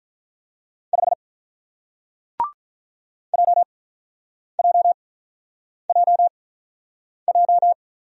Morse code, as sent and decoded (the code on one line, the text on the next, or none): ..... / .--- .--- .--- .---
5 JJJJ